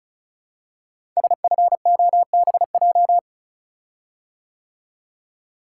Morse code, heard as a keyed tone, about 36 wpm, 700 Hz. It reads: SFOBJ
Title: Code Groups - 5 Letters - 35wpm